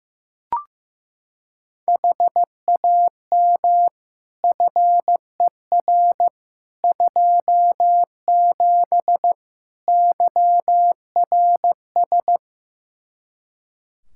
Morse code, heard as a keyed tone, about 15 wpm, 700 Hz. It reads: HAM FER 27 YRS
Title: Practice QSO - Exchange 3 - More Info - 15wpm